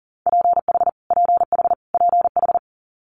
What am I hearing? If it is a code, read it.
Morse code, heard as a keyed tone, about 40 wpm, 700 Hz. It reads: PH PH PH
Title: Multi-letter Phonemes Repeated 3x 40wpm